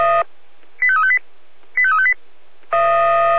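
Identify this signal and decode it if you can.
Five-tone selective call (ZVEI), heard as quick runs of five tones, tones of about 70 ms. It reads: 85358 85358
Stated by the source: ZVEI five-tone call received on an SDR, channel 496.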